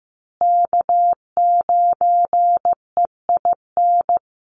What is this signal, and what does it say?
Morse code, heard as a keyed tone, about 15 wpm, 700 Hz. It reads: K9EIN
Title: USA Call Signs Repeated 2x 15wpm